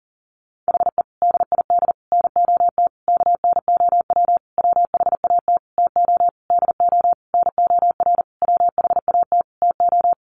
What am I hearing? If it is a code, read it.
Morse code, heard as a keyed tone, about 40 wpm, 700 Hz. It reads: HE DID NOT KNOW WHAT TO DO NOR WHAT TO
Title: Sentences from Top 1500 Words 40wpm